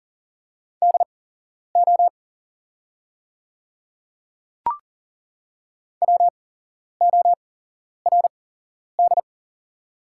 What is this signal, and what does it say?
Morse code, heard as a keyed tone, about 37 wpm, spaced as for 11 wpm, 700 Hz. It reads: DO WORD
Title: 8x Character Spacing - Top 100 Words - 40wpm